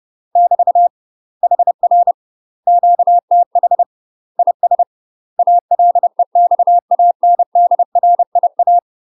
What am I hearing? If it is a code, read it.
Morse code, heard as a keyed tone, about 30 wpm, 700 Hz. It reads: <BT> HR QTH IS ALEXANDRIA
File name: UK MASTER QSOs TRAINING / 30WPM / UK MASTER QSOs 0033